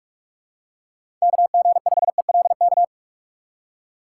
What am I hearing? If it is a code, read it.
Morse code, heard as a keyed tone, about 46 wpm, 700 Hz. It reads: KG5ELK
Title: USA Call Signs - 4x Character Spacing - 45wpm